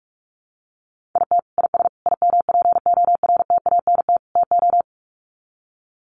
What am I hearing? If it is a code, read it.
Morse code, heard as a keyed tone, about 44 wpm, 700 Hz. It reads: IT IS IMPORTANT TO
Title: Sets of 4 Words 45wpm